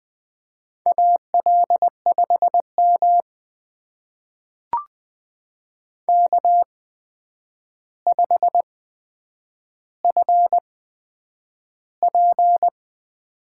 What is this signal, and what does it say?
Morse code, heard as a keyed tone, about 20 wpm, spaced as for 9 wpm, 700 Hz. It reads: AL5M K5FP
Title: USA Call Signs - 8x Character Spacing - 20wpm